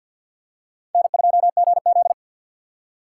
Morse code, heard as a keyed tone, about 50 wpm, 700 Hz. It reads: N2GZ
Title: USA Call Signs - 6x Character Spacing - 50wpm